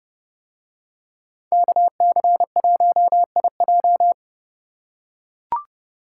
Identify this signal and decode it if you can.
Morse code, heard as a keyed tone, about 29 wpm, 700 Hz. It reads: KC1IJ